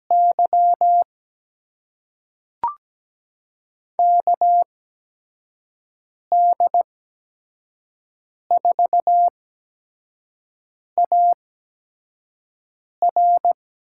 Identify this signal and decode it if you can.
Morse code, heard as a keyed tone, about 17 wpm, spaced as for 5 wpm, 700 Hz. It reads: Y KD4AR
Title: USA Call Signs - 6 Characters - 8x Character Spacing - 17wpm